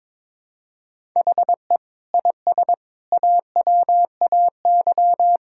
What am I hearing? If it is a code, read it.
Morse code, heard as a keyed tone, about 22 wpm, 700 Hz. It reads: HE IS AWAY
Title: Sentences from Top 1000 Words - Review Part 1 - 22wpm